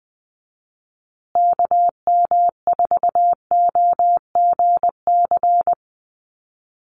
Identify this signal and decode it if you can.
Morse code, heard as a keyed tone, about 20 wpm, 700 Hz. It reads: KM4OGC
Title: USA Call Signs Repeated 3x 20wpm